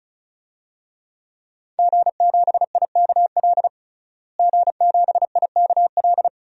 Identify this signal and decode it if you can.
Morse code, heard as a keyed tone, about 36 wpm, 700 Hz. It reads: G7IKL G7IKL
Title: DXCC-100 Call Signs - 5 Characters - Repeated 2x - 35wpm